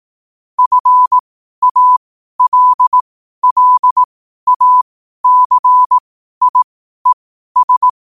Morse code, heard as a keyed tone, about 18 wpm, spaced as for 12 wpm, 1000 Hz. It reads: FALLACIES
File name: 12wpm